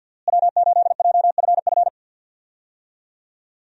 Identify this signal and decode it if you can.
Morse code, heard as a keyed tone, about 51 wpm, 700 Hz. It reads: W8JFF
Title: USA Call Signs - 5 Characters - Repeated 3x - 50wpm